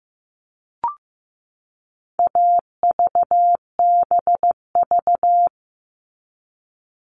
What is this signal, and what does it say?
Morse code, heard as a keyed tone, about 15 wpm, 700 Hz. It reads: AVBV
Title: All Cut Numbers - 4-Digit 15wpm